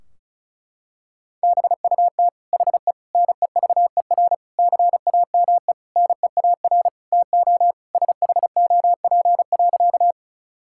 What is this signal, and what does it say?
Morse code, heard as a keyed tone, about 34 wpm, 700 Hz. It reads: BUT HE NEVER CAME NEAR TO SHOP.
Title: Sentences from Top 1600 Words 35wpm